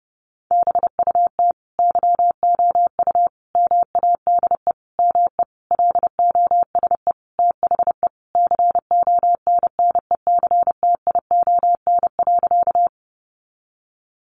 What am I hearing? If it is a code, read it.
Morse code, heard as a keyed tone, about 30 wpm, 700 Hz. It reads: BUT YOU MADE ME LOSE THE CONNECTION.
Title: Sentences from Top 1300 Words 30wpm